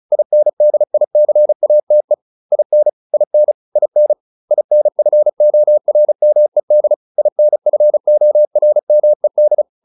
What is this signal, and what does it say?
Morse code, heard as a keyed tone, about 34 wpm, 600 Hz. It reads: INDICATE IN IN IN INFORMED INFORMED